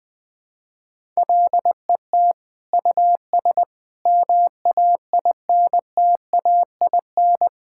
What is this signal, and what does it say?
Morse code, heard as a keyed tone, about 20 wpm, 700 Hz. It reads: LET US MAINTAIN
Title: Sentences from Top 1000 Words - Review Part 5 - 20wpm